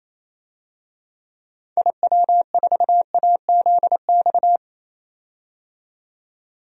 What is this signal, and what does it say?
Morse code, heard as a keyed tone, about 27 wpm, 700 Hz. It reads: IW4AZX
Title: DXCC-100 Call Signs - 6 Characters - 28wpm